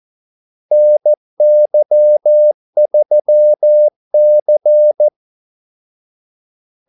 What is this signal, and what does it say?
Morse code, heard as a keyed tone, about 14 wpm, 600 Hz. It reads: NY3C
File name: Callsigns-14wpm